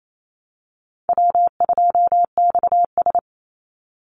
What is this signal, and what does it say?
Morse code, heard as a keyed tone, about 28 wpm, 700 Hz. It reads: W2XS
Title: Call Signs Repeated 3x - 28wpm